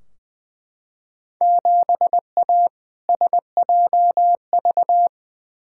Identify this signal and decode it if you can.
Morse code, heard as a keyed tone, about 21 wpm, 700 Hz. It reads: 7A SJV
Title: Field Day Exchanges Encore - 20wpm